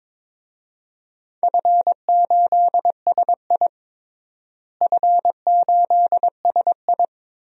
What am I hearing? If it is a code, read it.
Morse code, heard as a keyed tone, about 22 wpm, 700 Hz. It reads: F8SI F8SI
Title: DXCC-100 Call Signs - 4 Characters - Repeated 2x - 22wpm